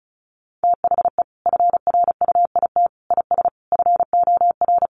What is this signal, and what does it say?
Morse code, heard as a keyed tone, about 34 wpm, 700 Hz. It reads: THE FRUIT IS FOR
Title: Sentences from Top 1300 Words 35wpm